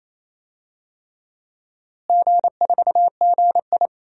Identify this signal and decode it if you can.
Morse code, heard as a keyed tone, about 28 wpm, 700 Hz. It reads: G4GI
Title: DXCC-100 Call Signs - 4 Characters - Repeated 3x - 28wpm